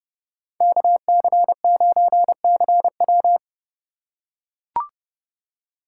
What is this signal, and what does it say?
Morse code, heard as a keyed tone, about 30 wpm, 700 Hz. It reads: KC9CW
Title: USA Call Signs - 2x Character Spacing - 30wpm